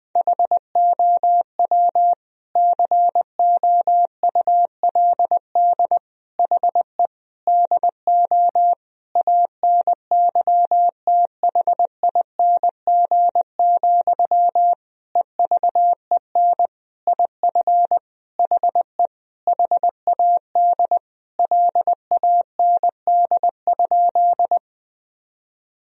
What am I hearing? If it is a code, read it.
Morse code, heard as a keyed tone, about 21 wpm, 700 Hz. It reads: HOW COULD HE DO ANYTHING, EVEN IF HE HAD LAND?
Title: Sentences from Top 1000 Words - Review Part 1 - 20wpm